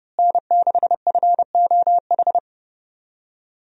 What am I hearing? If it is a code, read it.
Morse code, heard as a keyed tone, about 30 wpm, 700 Hz. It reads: N6FOH
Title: USA Call Signs - 5 Characters - Repeated 3x - 30wpm